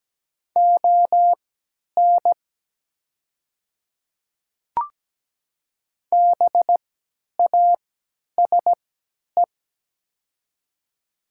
Morse code, heard as a keyed tone, about 17 wpm, spaced as for 10 wpm, 700 Hz. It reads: ON BASE
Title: 3x Character Spacing - Top 500 Words - 17wpm